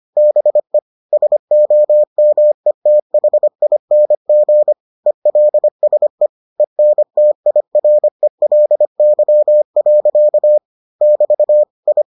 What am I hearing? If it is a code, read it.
Morse code, heard as a keyed tone, about 25 wpm, 600 Hz. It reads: BE SOMETHING ELSE ENTIRELY. <BT> I